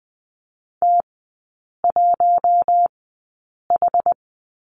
Morse code, heard as a keyed tone, about 20 wpm, spaced as for 8 wpm, 690 Hz. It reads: T1H
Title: 060 Lesson 11e - Letter h - Three Characters 20wpm